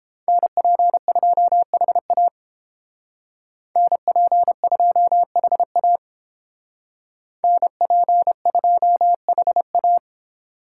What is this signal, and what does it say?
Morse code, heard as a keyed tone, about 32 wpm, 700 Hz. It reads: NP2HA NP2HA NP2HA